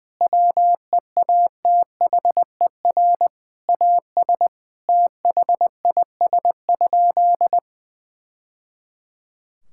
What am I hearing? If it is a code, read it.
Morse code, heard as a keyed tone, about 21 wpm, 700 Hz. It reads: WEATHER AS THIS?
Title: Sentences from Top 1000 Words - Review Part 5 - 20wpm